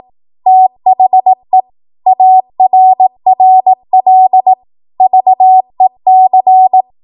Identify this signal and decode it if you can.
Morse code, heard as a keyed tone, about 18 wpm, 750 Hz. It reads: THE ARRL VEC